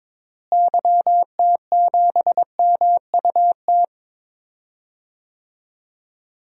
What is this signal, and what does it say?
Morse code, heard as a keyed tone, about 22 wpm, 700 Hz. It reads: YT7MUT
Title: DXCC-100 Call Signs - 6 Characters - Repeated 3x - 22wpm